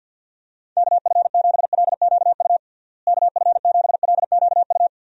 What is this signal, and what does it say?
Morse code, heard as a keyed tone, about 52 wpm, 700 Hz. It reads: KF7LQU KF7LQU
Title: USA Call Signs - 6 Characters - Repeated 2x - 50wpm